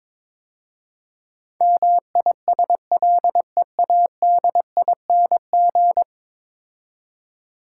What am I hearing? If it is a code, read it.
Morse code, heard as a keyed tone, about 22 wpm, 700 Hz. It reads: MISLEADING
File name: practice / 22wpm / 10 Letter Words - First 3 Letters Given - 22wpm